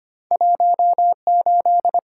Morse code, heard as a keyed tone, about 25 wpm, 690 Hz. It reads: 18
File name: Two Letter-Number Combinations - 25wpm